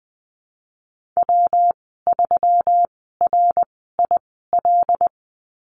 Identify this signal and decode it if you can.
Morse code, heard as a keyed tone, about 20 wpm, spaced as for 15 wpm, 700 Hz. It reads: W3RIL
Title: 122 Lesson 20g - Number 3 - Call Signs 20wpm